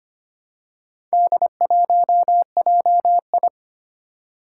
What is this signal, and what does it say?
Morse code, heard as a keyed tone, about 25 wpm, 700 Hz. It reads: D1JI
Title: Four Challenging Letter-Number Combinations 25wpm